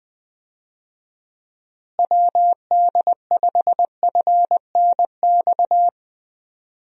Morse code, heard as a keyed tone, about 20 wpm, 700 Hz. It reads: WD5FNX
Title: USA Call Signs - 6 Characters - Repeated 2x - 20wpm